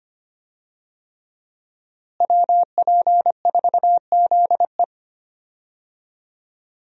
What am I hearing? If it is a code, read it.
Morse code, heard as a keyed tone, about 25 wpm, 700 Hz. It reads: WP4ZE
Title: USA Call Signs - 5 Characters - Repeated 3x - 25wpm